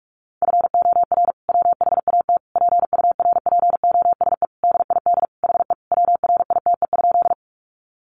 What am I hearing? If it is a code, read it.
Morse code, heard as a keyed tone, about 44 wpm, 700 Hz. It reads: FOR WHAT PURPOSE DID HE WRITE?
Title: Sentences from Top 800 Words 45wpm